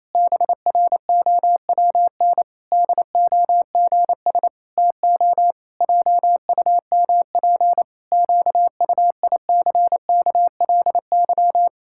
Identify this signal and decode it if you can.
Morse code, heard as a keyed tone, about 28 wpm, 700 Hz. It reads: BROWN DOGS TO JUMP QUICKLY
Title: Pangram Challenge v2 - 28wpm